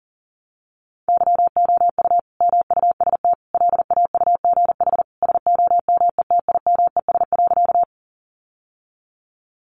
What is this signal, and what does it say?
Morse code, heard as a keyed tone, about 40 wpm, 700 Hz. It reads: YOU MUST LAUGH SOMETIMES.
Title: Sentences from Top 600 Words 40wpm